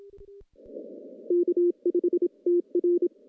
Morse code, heard as a keyed tone, about 27 wpm, 350 Hz. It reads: K5TR
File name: contests / CW-SS-k5tr- -0000